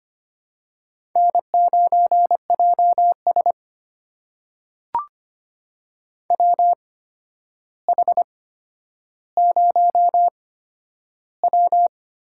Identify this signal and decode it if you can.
Morse code, heard as a keyed tone, about 25 wpm, spaced as for 11 wpm, 700 Hz. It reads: N9JS WH0W
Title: USA Call Signs - 4 Characters - 8x Character Spacing - 25wpm